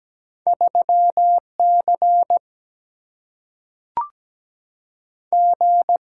Morse code, heard as a keyed tone, about 17 wpm, 700 Hz. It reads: 3C G
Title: Single-Letter Single-Number 17wpm